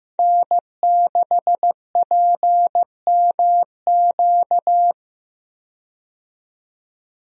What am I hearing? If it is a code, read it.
Morse code, heard as a keyed tone, about 15 wpm, 700 Hz. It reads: N6PMQ
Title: USA Call Signs - 5 Characters - 15wpm